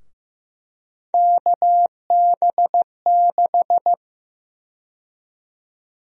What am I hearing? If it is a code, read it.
Morse code, heard as a keyed tone, about 15 wpm, 700 Hz. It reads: KB6
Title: USA Call Sign Prefixes - Repeated 2x - 15wpm